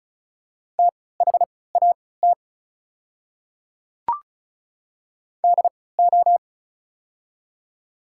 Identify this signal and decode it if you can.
Morse code, heard as a keyed tone, about 36 wpm, spaced as for 20 wpm, 700 Hz. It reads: THAT DO
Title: 3x Character Spacing - Top 100 Words - 35wpm